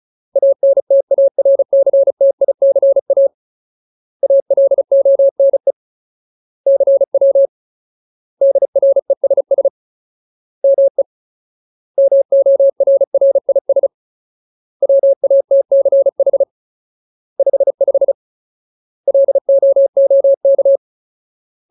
Morse code, heard as a keyed tone, about 34 wpm, 550 Hz. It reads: ANTARCTICA ALONE CW DRESS ME MORRIS WATCH 55 LOOK
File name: Ultimate Word List / 35WPM Ultimate